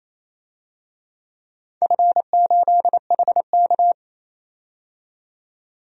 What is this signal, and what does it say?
Morse code, heard as a keyed tone, about 28 wpm, 700 Hz. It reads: F8HK
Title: DXCC-100 Call Signs - 4 Characters - 28wpm